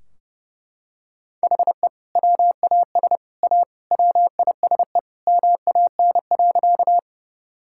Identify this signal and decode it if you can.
Morse code, heard as a keyed tone, about 30 wpm, 700 Hz. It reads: HE WAS A WISE MAN.
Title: Sentences from Top 1600 Words 30wpm